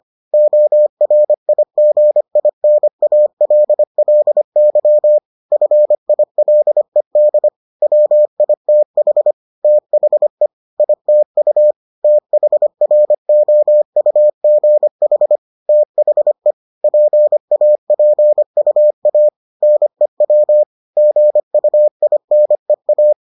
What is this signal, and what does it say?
Morse code, heard as a keyed tone, about 25 wpm, 600 Hz. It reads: ORIGINALLY FILED WITH THE ITU THROUGH THE PAPUA NEW GUINEA